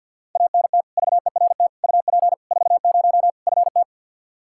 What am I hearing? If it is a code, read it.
Morse code, heard as a keyed tone, about 50 wpm, 700 Hz. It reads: ANT VERT UP 40 FT
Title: Practice QSO - Exchange 2 - Info - 50wpm